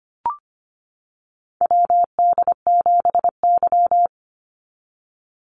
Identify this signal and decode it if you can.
Morse code, heard as a keyed tone, about 25 wpm, 700 Hz. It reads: WD7Y
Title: NA SOTA Chaser Callsigns 25wpm